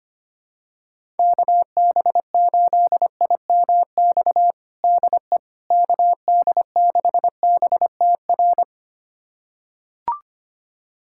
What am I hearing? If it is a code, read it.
Morse code, heard as a keyed tone, about 25 wpm, 700 Hz. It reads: KB8IMX DE KD6BTR
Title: USA Callsign DE Callsign 25wpm